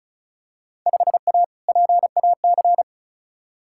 Morse code, heard as a keyed tone, about 36 wpm, 700 Hz. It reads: 5A PAC
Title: Field Day Exchanges Encore - 35wpm